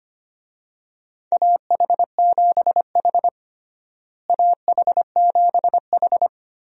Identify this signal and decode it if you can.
Morse code, heard as a keyed tone, about 25 wpm, 700 Hz. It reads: AH7H AH7H
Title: USA Call Signs - 4 Characters - Repeated 3x - 25wpm